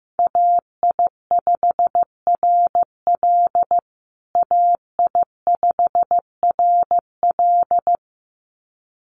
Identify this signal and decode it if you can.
Morse code, heard as a keyed tone, about 15 wpm, 700 Hz. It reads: AI5RL AI5RL
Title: USA Call Signs Repeated 2x 15wpm